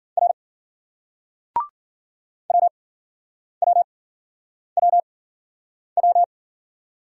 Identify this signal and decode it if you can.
Morse code, heard as a keyed tone, about 50 wpm, spaced as for 9 wpm, 700 Hz. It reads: R WWWW